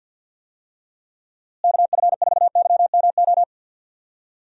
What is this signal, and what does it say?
Morse code, heard as a keyed tone, about 52 wpm, 700 Hz. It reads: KF4YMO
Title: USA Call Signs - 6 Characters - Repeated 3x - 50wpm